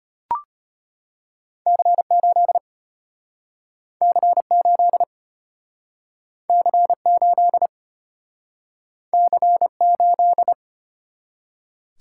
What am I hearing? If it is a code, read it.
Morse code, heard as a keyed tone, about 34 wpm, 700 Hz. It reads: C8 C8 C8 C8